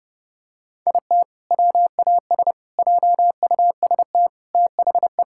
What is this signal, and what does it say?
Morse code, heard as a keyed tone, about 30 wpm, 700 Hz. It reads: IT WAS JUST THE
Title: Sentences from Top 1000 Words - Review Part 5 - 30wpm